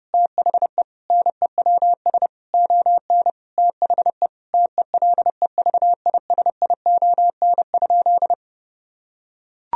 Morse code, heard as a keyed tone, about 30 wpm, 700 Hz. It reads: THE NEWS ON THE TELEVISION?
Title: Sentences from Top 1000 Words - Review Part 4 - 30wpm